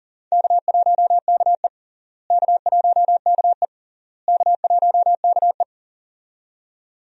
Morse code, heard as a keyed tone, about 40 wpm, 700 Hz. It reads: K1KE K1KE K1KE
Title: USA Call Signs - 4 Characters - Repeated 3x - 40wpm